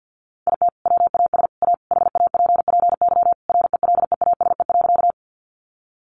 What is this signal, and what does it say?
Morse code, heard as a keyed tone, about 49 wpm, 700 Hz. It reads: IT WAS A HAPPY RELEASE.
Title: Sentences from Top 1100 Words 50wpm